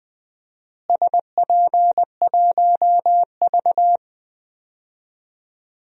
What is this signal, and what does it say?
Morse code, heard as a keyed tone, about 21 wpm, 700 Hz. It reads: SP1V
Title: DXCC-100 Call Signs - 4 Characters - Repeated 3x - 20wpm